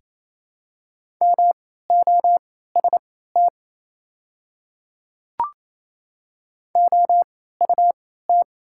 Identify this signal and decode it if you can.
Morse code, heard as a keyed tone, about 27 wpm, spaced as for 16 wpm, 700 Hz. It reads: MOST OUT
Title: 3x Character Spacing - Top 200 Words - 28wpm